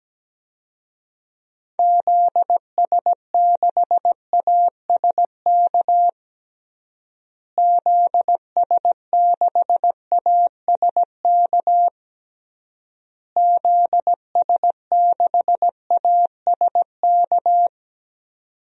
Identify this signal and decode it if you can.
Morse code, heard as a keyed tone, about 17 wpm, 700 Hz. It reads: ZS6ASK ZS6ASK ZS6ASK
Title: DXCC-100 Call Signs - 6 Characters - Repeated 3x - 17wpm